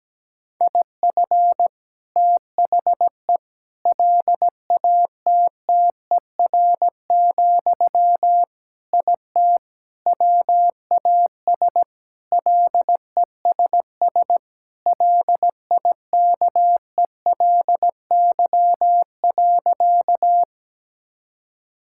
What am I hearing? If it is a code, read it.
Morse code, heard as a keyed tone, about 17 wpm, 700 Hz. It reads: IF THE LATTER, IT WAS LESS LIKELY.
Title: Sentences from Top 1600 Words 17wpm